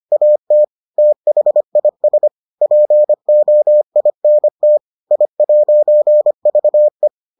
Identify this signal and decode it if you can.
Morse code, heard as a keyed tone, about 25 wpm, 600 Hz. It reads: AT THIS POINT I'VE